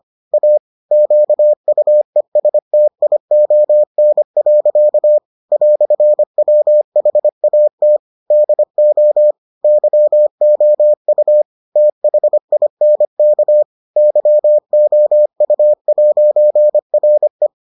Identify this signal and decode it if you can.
Morse code, heard as a keyed tone, about 25 wpm, 600 Hz. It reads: A QUESTION. "WHAT DO YOU THINK YOU'RE